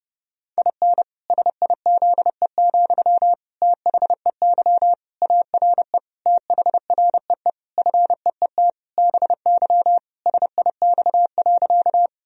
Morse code, heard as a keyed tone, about 32 wpm, 700 Hz. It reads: IN SIZE, THEY ARE THREE FEET BY SIX.
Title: Sentences from Top 1000 Words - Review Part 2 - 30wpm